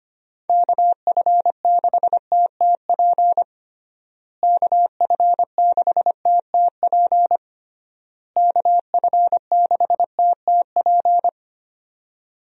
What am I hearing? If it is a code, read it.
Morse code, heard as a keyed tone, about 25 wpm, 700 Hz. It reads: KF6TTP KF6TTP KF6TTP
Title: USA Call Signs - 6 Characters - Repeated 3x - 25wpm